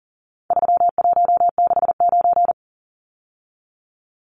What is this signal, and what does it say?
Morse code, heard as a keyed tone, about 40 wpm, 700 Hz. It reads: 3169
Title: Numbers - 4-Digit 40wpm